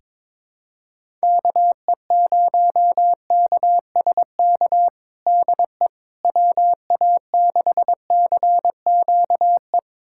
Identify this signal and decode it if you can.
Morse code, heard as a keyed tone, about 22 wpm, 700 Hz. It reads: KE0KSK DE WA6CQE
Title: USA Callsign DE Callsign 22wpm